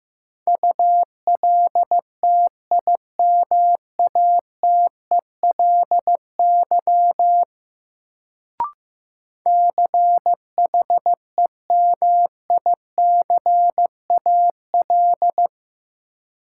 Morse code, heard as a keyed tone, about 15 wpm, 700 Hz. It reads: ULTIMATELY CHEMICAL
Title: Sentences from Top 1200 Words - New Word Drill - 15wpm